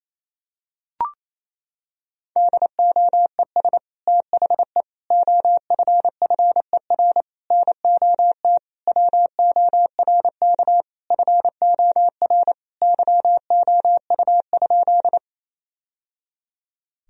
Morse code, heard as a keyed tone, about 28 wpm, 700 Hz. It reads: DOES THE OFFER NOT WORK FOR YOU?
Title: Sentences from Top 1000 Words - Review Part 4 - 28wpm